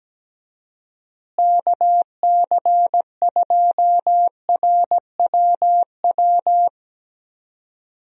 Morse code, heard as a keyed tone, about 17 wpm, 700 Hz. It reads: KC2RWW
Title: USA Call Signs - 6 Characters - Repeated 2x - 17wpm